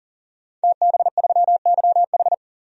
Morse code, heard as a keyed tone, about 40 wpm, 700 Hz. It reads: TB3YH
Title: DXCC-100 Call Signs - 5 Characters - 40wpm